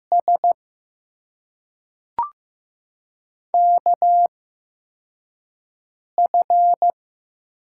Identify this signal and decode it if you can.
Morse code, heard as a keyed tone, about 15 wpm, spaced as for 3 wpm, 700 Hz. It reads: SKF
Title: USA Call Signs - 6 Characters - 8x Character Spacing - 15wpm